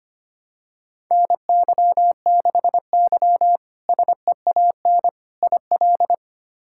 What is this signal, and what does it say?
Morse code, heard as a keyed tone, about 25 wpm, 700 Hz. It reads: NY6Y SEAN IL
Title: K1USN SST QSO Practice - 25wpm